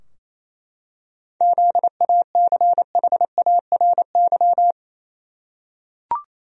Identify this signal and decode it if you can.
Morse code, heard as a keyed tone, about 28 wpm, 700 Hz. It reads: ZACHARY
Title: USA Common Names - 28wpm